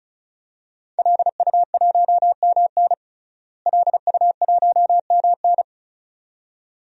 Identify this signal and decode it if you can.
Morse code, heard as a keyed tone, about 34 wpm, 700 Hz. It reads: LU1MN LU1MN
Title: DXCC-100 Call Signs - 5 Characters - Repeated 2x - 35wpm